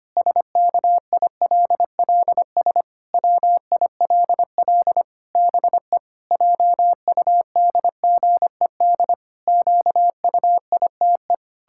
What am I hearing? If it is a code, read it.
Morse code, heard as a keyed tone, about 25 wpm, 700 Hz. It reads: SKILLS WILL BE JUDGED QUITE
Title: Pangram Challenge v2 - 25wpm